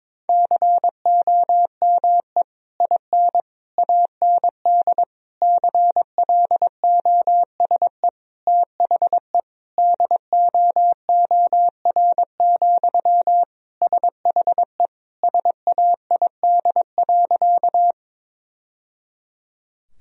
Morse code, heard as a keyed tone, about 22 wpm, 700 Hz. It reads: COME IN AND CLOSE THE DOOR, SHE SAID.
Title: Sentences from Top 1000 Words - Review Part 1 - 22wpm